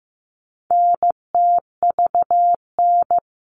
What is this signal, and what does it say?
Morse code, heard as a keyed tone, about 15 wpm, 700 Hz. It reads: NTVN
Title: All Cut Numbers - 4-Digit 15wpm